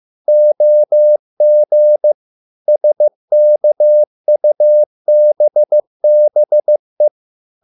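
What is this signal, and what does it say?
Morse code, heard as a keyed tone, about 15 wpm, 600 Hz. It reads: OG SKUBBE
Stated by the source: Morse code at a slow 15 wpm from CW med Gnister.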